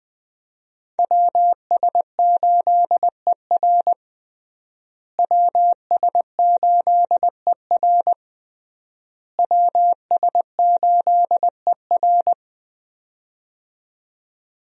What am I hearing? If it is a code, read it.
Morse code, heard as a keyed tone, about 20 wpm, 700 Hz. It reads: WS8ER WS8ER WS8ER
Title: USA Call Signs - 5 Characters - Repeated 3x - 20wpm